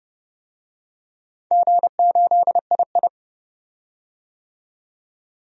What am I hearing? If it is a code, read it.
Morse code, heard as a keyed tone, about 32 wpm, 700 Hz. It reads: G8II
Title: DXCC-100 Call Signs - 4 Characters - 30wpm